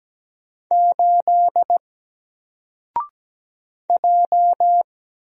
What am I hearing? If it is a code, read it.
Morse code, heard as a keyed tone, about 17 wpm, 700 Hz. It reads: 8 J
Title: Single Letter-Number Alt 17wpm